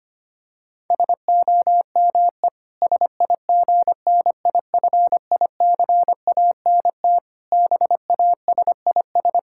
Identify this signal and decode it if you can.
Morse code, heard as a keyed tone, about 25 wpm, 700 Hz. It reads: SOME SIGNIFICANT BASIS
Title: Sentences from Top 1600 Words 25wpm